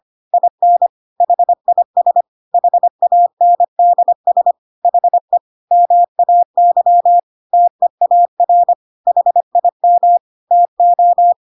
Morse code, heard as a keyed tone, about 25 wpm, 700 Hz. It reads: IN HIS HANDS HE MAY TEAR HIM TO
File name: Pinocchio / Pinocchio ch3 25 25 ik1hge